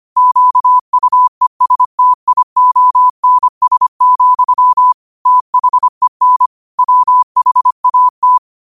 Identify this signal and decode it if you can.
Morse code, heard as a keyed tone, about 25 wpm, 1000 Hz. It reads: QUESTIONS, THEN WHAT
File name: qotdfiles / 25wpm